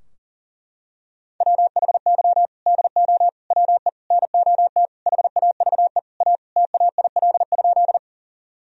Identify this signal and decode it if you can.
Morse code, heard as a keyed tone, about 40 wpm, 700 Hz. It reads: WHY DO WE NOT HAVE A TAIL?
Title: Sentences from Top 1000 Words - Review Part 3 - 40wpm